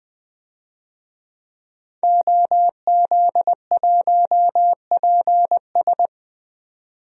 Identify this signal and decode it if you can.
Morse code, heard as a keyed tone, about 20 wpm, 700 Hz. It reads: OZ1PS
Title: DXCC-100 Call Signs - 5 Characters - 20wpm